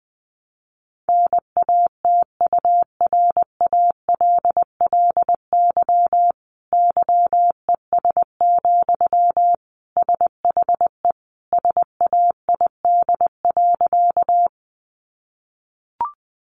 Morse code, heard as a keyed tone, about 20 wpm, 700 Hz. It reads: NATURALLY YES, SHE SAID.
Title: Sentences from Top 1300 Words 20wpm